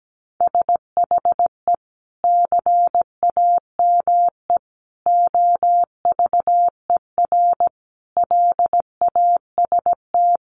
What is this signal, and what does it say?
Morse code, heard as a keyed tone, about 17 wpm, 700 Hz. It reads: SHE CAME OVER LAST
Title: Sentences from Top 300 Words 17wpm